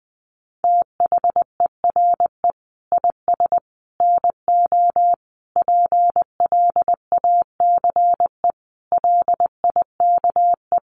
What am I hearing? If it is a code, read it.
Morse code, heard as a keyed tone, about 20 wpm, 700 Hz. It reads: THERE IS NO PLACE LIKE
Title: Sentences from Top 200 Words 20wpm